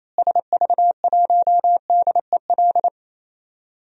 Morse code, heard as a keyed tone, about 27 wpm, 700 Hz. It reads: SV1DEL
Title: DXCC-100 Call Signs - 6 Characters - Repeated 2x - 28wpm